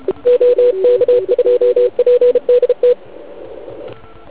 Morse code, heard as a keyed tone, about 34 wpm, 490 Hz. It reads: OK2PDT
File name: Archiv / CUC149 / OK1IF CUC149